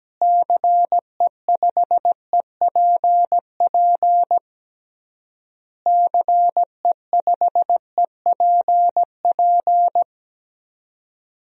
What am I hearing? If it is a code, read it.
Morse code, heard as a keyed tone, about 17 wpm, 700 Hz. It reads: CE5EPP CE5EPP
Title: DXCC-100 Call Signs - 6 Characters - Repeated 3x - 17wpm